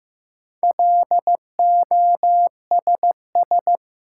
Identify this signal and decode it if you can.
Morse code, heard as a keyed tone, about 15 wpm, 700 Hz. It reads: LOSS
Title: Four Letter Words - 15wpm